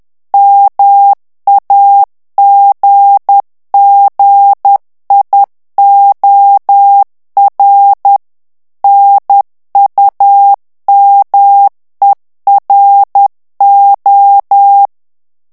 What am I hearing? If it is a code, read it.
Morse code, heard as a keyed tone, about 10 wpm, 800 Hz. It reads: MAGGIOR NUMERO
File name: ITALIEN texte clair 2 vitesse 10 mots minute